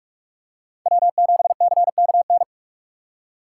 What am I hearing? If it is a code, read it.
Morse code, heard as a keyed tone, about 48 wpm, 700 Hz. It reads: W7CKN
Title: USA Call Signs - 5 Characters - 4x Character Spacing - 45wpm